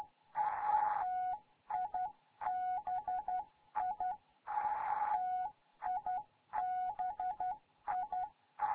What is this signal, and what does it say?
Morse code, heard as a keyed tone, about 12 wpm, 740 Hz. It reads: TIBI TIBI
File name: Morze-HA2MN